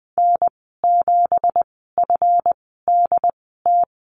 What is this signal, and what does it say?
Morse code, heard as a keyed tone, about 20 wpm, spaced as for 14 wpm, 700 Hz. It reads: N7FDT
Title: 219 Lesson 35g - Number 0 - Call Signs 20wpm